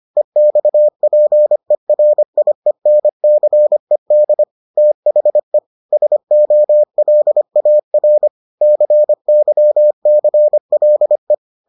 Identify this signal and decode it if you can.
Morse code, heard as a keyed tone, about 25 wpm, 600 Hz. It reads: EXPERIENCED THE SOLAR CYCLE